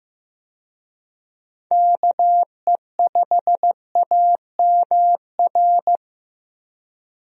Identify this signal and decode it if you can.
Morse code, heard as a keyed tone, about 16 wpm, 700 Hz. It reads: KE5AMR
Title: USA Call Signs - 6 Characters - Repeated 3x - 15wpm